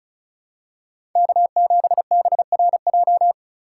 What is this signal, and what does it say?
Morse code, heard as a keyed tone, about 36 wpm, 700 Hz. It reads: K7BRJ
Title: USA Call Signs - 5 Characters - 3x Character Spacing - 35wpm